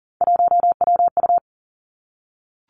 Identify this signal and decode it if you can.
Morse code, heard as a keyed tone, about 39 wpm, 700 Hz. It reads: 1WU
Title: Three Challenging Letter-Number Combinations 40wpm